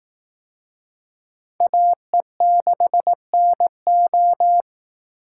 Morse code, heard as a keyed tone, about 18 wpm, 700 Hz. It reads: AE6NO
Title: Speed Racing - USA 2x2 Call Signs 15wpm